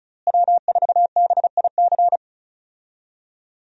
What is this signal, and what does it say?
Morse code, heard as a keyed tone, about 35 wpm, 700 Hz. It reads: W4BIC
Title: USA Call Signs - 5 Characters - Repeated 3x - 35wpm